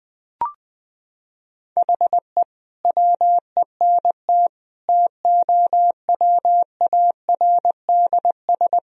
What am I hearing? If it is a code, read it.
Morse code, heard as a keyed tone, about 20 wpm, 700 Hz. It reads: HE WENT TOWARDS
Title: Sentences from Top 1000 Words - Review Part 2 - 20wpm